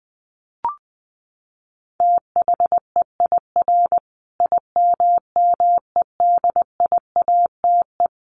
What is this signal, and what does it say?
Morse code, heard as a keyed tone, about 20 wpm, 700 Hz. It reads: THEIR IMMEDIATE
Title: Sentences from Top 1200 Words 20wpm